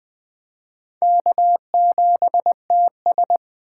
Morse code, heard as a keyed tone, about 20 wpm, 700 Hz. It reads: K7TS
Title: Speed Racing - USA 1x2 Call Signs 17wpm